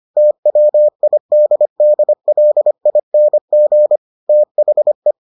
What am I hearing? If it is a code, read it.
Morse code, heard as a keyed tone, about 25 wpm, 600 Hz. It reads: TWIDDLING THE